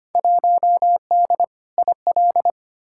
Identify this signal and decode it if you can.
Morse code, heard as a keyed tone, about 25 wpm, 700 Hz. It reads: 1D IL
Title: Field Day Exchanges - 25wpm